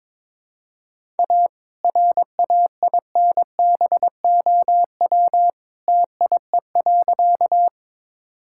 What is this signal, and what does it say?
Morse code, heard as a keyed tone, about 22 wpm, 700 Hz. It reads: A RAINBOW TIE.
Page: morse-code-ninja-practice-22wpm